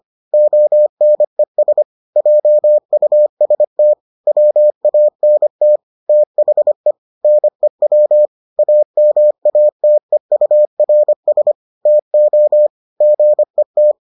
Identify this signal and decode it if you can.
Morse code, heard as a keyed tone, about 25 wpm, 600 Hz. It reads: ONES JUST WANT THE NEW AMATEURS TO GET